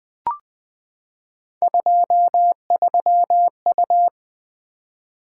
Morse code, heard as a keyed tone, about 20 wpm, 700 Hz. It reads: 23U
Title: Three Letter-Number Combinations - 20wpm